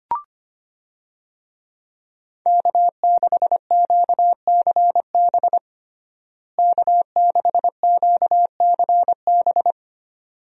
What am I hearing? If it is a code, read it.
Morse code, heard as a keyed tone, about 25 wpm, 700 Hz. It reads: K6QCB K6QCB
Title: USA Call Signs - 5 Characters - Repeated 2x - 25wpm